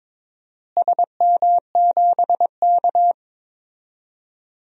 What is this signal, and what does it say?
Morse code, heard as a keyed tone, about 21 wpm, 700 Hz. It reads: SM7K
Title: DXCC-100 Call Signs - 4 Characters - Repeated 2x - 22wpm